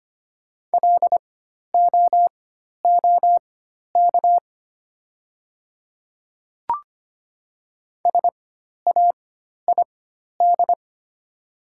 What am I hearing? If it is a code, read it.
Morse code, heard as a keyed tone, about 25 wpm, spaced as for 12 wpm, 700 Hz. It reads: LOOK SAID
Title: 4x Character Spacing - Top 300 Words - 25wpm